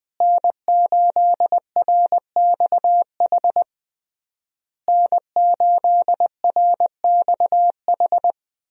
Morse code, heard as a keyed tone, about 20 wpm, 700 Hz. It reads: N8RXH N8RXH
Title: USA Call Signs - 5 Characters - Repeated 3x - 20wpm